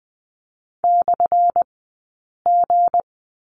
Morse code, heard as a keyed tone, about 20 wpm, spaced as for 8 wpm, 700 Hz. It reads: /G
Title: 177 Lesson 29d - Slash - Two Characters 20wpm